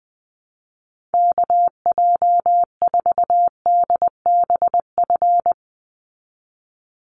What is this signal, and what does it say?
Morse code, heard as a keyed tone, about 20 wpm, 700 Hz. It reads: KJ4DBF
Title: USA Call Signs 20wpm